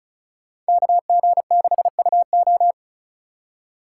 Morse code, heard as a keyed tone, about 36 wpm, 700 Hz. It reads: KG6UO
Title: USA Call Signs - 5 Characters - 3x Character Spacing - 35wpm